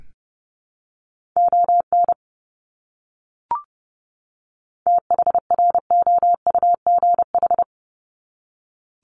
Morse code, heard as a keyed tone, about 30 wpm, 700 Hz. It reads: ON THROUGH
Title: Top 200 Words 30wpm